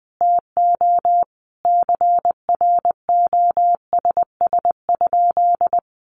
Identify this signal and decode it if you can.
Morse code, heard as a keyed tone, about 20 wpm, 700 Hz. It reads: TO CROSS?
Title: Sentences from Top 300 Words 20wpm